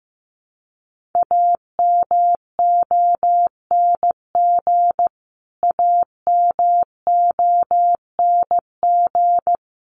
Morse code, heard as a keyed tone, about 15 wpm, 700 Hz. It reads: AMONG AMONG
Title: Top 500 Words Repeated 15wpm